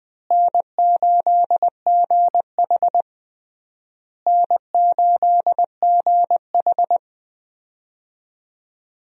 Morse code, heard as a keyed tone, about 20 wpm, 700 Hz. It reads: N8GH N8GH
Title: USA Call Signs - 4 Characters - Repeated 3x - 20wpm